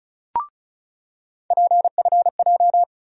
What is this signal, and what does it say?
Morse code, heard as a keyed tone, about 36 wpm, 700 Hz. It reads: PFJ
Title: USA Call Sign Postfixes - 3 Letters 35wpm